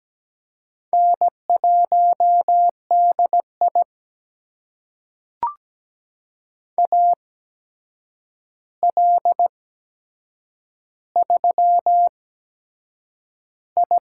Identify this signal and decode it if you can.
Morse code, heard as a keyed tone, about 17 wpm, spaced as for 7 wpm, 700 Hz. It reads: N1DI AL3I
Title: USA Call Signs - 4 Characters - 8x Character Spacing - 17wpm